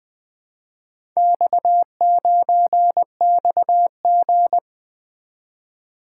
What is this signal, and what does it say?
Morse code, heard as a keyed tone, about 20 wpm, 700 Hz. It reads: X9XG
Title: Four Letter-Number Combinations - 20wpm